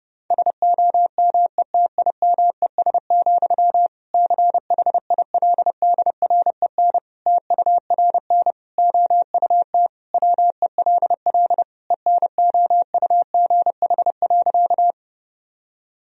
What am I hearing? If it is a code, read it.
Morse code, heard as a keyed tone, about 30 wpm, 700 Hz. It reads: SOMETIMES, CHILDREN TURN OUT WELL ENOUGH.
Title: Sentences from Top 1000 Words - Review Part 3 - 30wpm